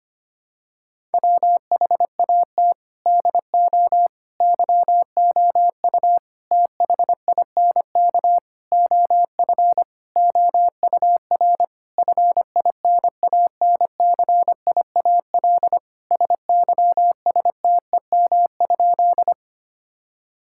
Morse code, heard as a keyed tone, about 25 wpm, 700 Hz. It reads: WHAT DO YOU THINK OF OUR FINANCIAL SYSTEM?
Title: Sentences from Top 1000 Words - Review Part 5 - 25wpm